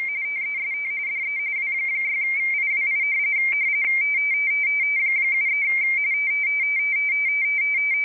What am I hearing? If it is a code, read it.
RTTY, about 45.45 baud. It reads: TEST DE W6IWI 73
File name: rtty / websdr recording 14086.0kHz